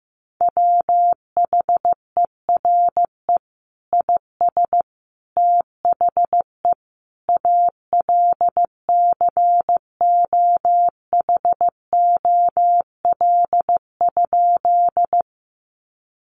Morse code, heard as a keyed tone, about 15 wpm, 700 Hz. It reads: WHERE IS THE ALCOHOL?
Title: Sentences from Top 1400 Words 15wpm